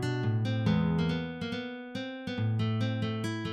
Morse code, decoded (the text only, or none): none